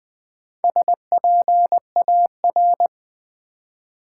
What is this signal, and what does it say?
Morse code, heard as a keyed tone, about 20 wpm, 700 Hz. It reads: SPAR
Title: Build A Word - 1 Letter At A Time - 5-Letter Words - 20wpm